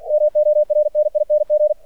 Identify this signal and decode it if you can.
Morse code, heard as a keyed tone, about 49 wpm, 600 Hz, with radio noise. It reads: MORNING
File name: archived / CRSnet